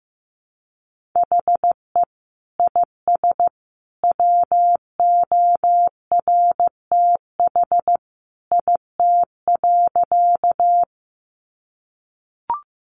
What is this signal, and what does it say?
Morse code, heard as a keyed tone, about 15 wpm, 700 Hz. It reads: HE IS WORTH IT.
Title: Sentences from Top 800 Words 15wpm